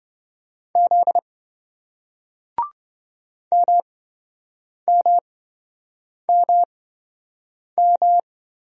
Morse code, heard as a keyed tone, about 26 wpm, spaced as for 7 wpm, 700 Hz. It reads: Z MMMM